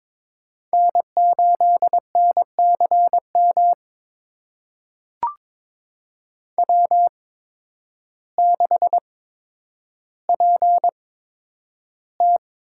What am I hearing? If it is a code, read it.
Morse code, heard as a keyed tone, about 21 wpm, spaced as for 10 wpm, 700 Hz. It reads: N8NCM W6PT
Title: USA Call Signs - 5 Characters - 8x Character Spacing - 22wpm